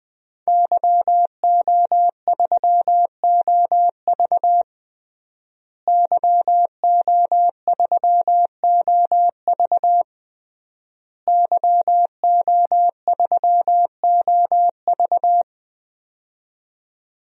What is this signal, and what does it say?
Morse code, heard as a keyed tone, about 21 wpm, 700 Hz. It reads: YO3OV YO3OV YO3OV
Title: DXCC-100 Call Signs - 5 Characters - Repeated 3x - 20wpm